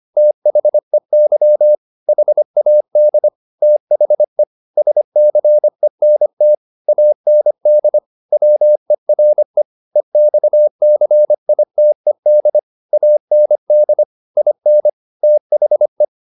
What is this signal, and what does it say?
Morse code, heard as a keyed tone, about 25 wpm, 600 Hz. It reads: THEY HAD THE SCENT AND WERE EXCITED AND IN THE